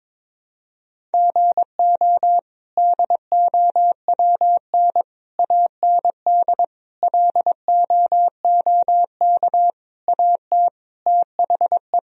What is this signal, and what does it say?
Morse code, heard as a keyed tone, about 22 wpm, 700 Hz. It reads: GO DOWN AND LOOK AT THE
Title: Sentences from Top 1000 Words - Review Part 1 - 22wpm